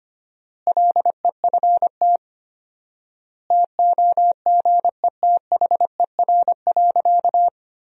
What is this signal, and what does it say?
Morse code, heard as a keyed tone, about 25 wpm, 700 Hz. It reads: LEFT TOGETHER.
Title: 4x Word-Spacing - Sentences from Top 400 Words - 25wpm